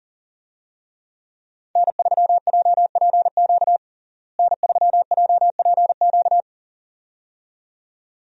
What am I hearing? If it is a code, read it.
Morse code, heard as a keyed tone, about 40 wpm, 700 Hz. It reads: N3JPQ N3JPQ
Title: USA Call Signs - 5 Characters - Repeated 2x - 40wpm